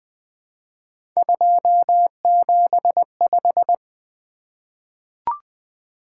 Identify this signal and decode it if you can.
Morse code, heard as a keyed tone, about 20 wpm, 700 Hz. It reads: 275
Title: Three Letter-Number Combinations - 20wpm